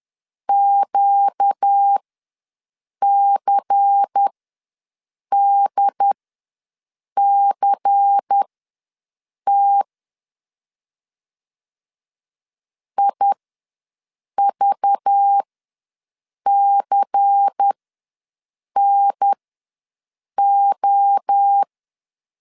Morse code, heard as a keyed tone, about 10 wpm, spaced as for 6 wpm, 800 Hz. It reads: QCDCT IVCNO
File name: LEZIONI / 5 LBVC / 5 LBVC ws 10wpm 3733